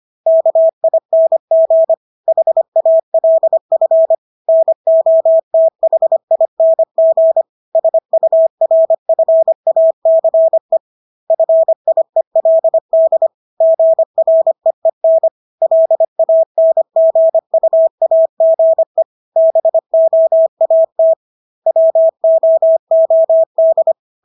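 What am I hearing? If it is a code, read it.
Morse code, heard as a keyed tone, about 25 wpm, 650 Hz. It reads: KING HALF NOTHING SURFACE FIELD GREEN LANGUAGE BOAT WOOD
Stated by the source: Morse practice of English words at 25 wpm from CW med Gnister.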